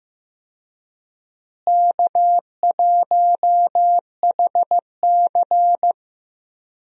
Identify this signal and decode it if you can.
Morse code, heard as a keyed tone, about 15 wpm, 700 Hz. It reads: K1HC
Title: USA Call Signs - 4 Characters - Repeated 2x - 15wpm